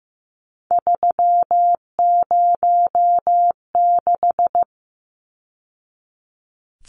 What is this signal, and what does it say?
Morse code, heard as a keyed tone, about 15 wpm, 700 Hz. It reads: 306
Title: Numbers - 3-Digit 15wpm